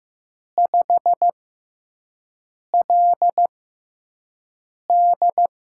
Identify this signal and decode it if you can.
Morse code, heard as a keyed tone, about 15 wpm, spaced as for 5 wpm, 700 Hz. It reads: 5LD
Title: USA Call Signs - 5 Characters - 6x Character Spacing - 15wpm